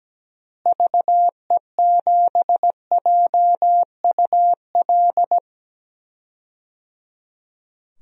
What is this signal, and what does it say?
Morse code, heard as a keyed tone, about 17 wpm, 700 Hz. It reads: VE7JUL
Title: DXCC-100 Call Signs - 6 Characters - Repeated 3x - 17wpm